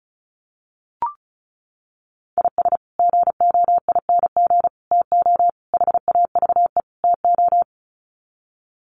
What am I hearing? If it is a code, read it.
Morse code, heard as a keyed tone, about 34 wpm, 700 Hz. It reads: IS GOING TO HAVE TO
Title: Sets of 5 Words 35wpm